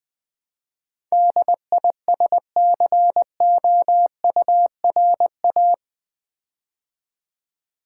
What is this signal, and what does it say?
Morse code, heard as a keyed tone, about 20 wpm, 700 Hz. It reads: DISCOURA
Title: Build A Word - 4 Letters At A Time - 10-Letter Words - 20wpm